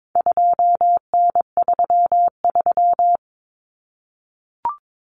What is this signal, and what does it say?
Morse code, heard as a keyed tone, about 22 wpm, 700 Hz. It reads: 2N33
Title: Common Cut Numbers - 4-Digit 22wpm